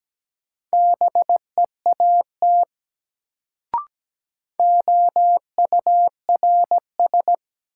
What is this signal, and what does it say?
Morse code, heard as a keyed tone, about 17 wpm, 700 Hz. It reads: BEAT OURS
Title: Sentences from Top 1300 Words - New Word Drill Alt - 17wpm